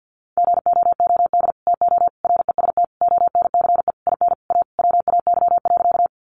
Morse code, heard as a keyed tone, about 50 wpm, 700 Hz. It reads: GOOD TO REST ONCE IN A WAY.
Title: Sentences from Top 500 Words 50wpm